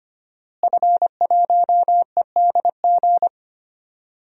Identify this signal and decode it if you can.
Morse code, heard as a keyed tone, about 25 wpm, 700 Hz. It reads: F1EDG
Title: DXCC-100 Call Signs - 5 Characters - Repeated 2x - 25wpm